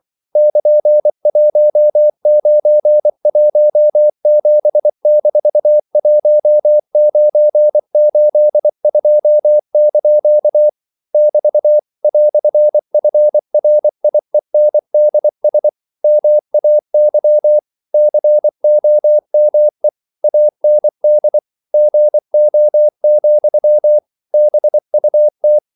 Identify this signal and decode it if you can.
Morse code, heard as a keyed tone, about 24 wpm, 600 Hz. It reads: <KN>1917-1982) <BT> "FRIENDS MAY COME AND GO, BUT
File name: CW / Practice / EnglishQuotations0324